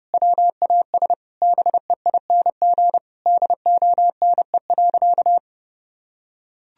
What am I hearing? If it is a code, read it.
Morse code, heard as a keyed tone, about 32 wpm, 700 Hz. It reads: WAS BEING DONE.
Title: Sentences from Top 1000 Words - Review Part 3 - 30wpm